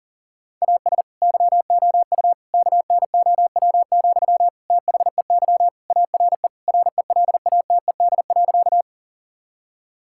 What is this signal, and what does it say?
Morse code, heard as a keyed tone, about 40 wpm, 700 Hz. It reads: AS YOU KNOW, THEY ARE RELATED.
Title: Sentences from Top 1000 Words - Review Part 5 - 40wpm